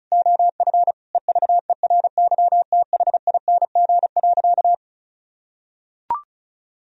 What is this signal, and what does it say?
Morse code, heard as a keyed tone, about 36 wpm, 700 Hz. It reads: OF EVERYTHING.
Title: Sentences from Top 1000 Words - Review Part 2 - 35wpm